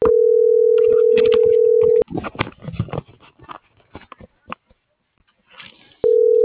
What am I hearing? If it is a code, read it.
Ringback tone (North America), 440 + 480 Hz.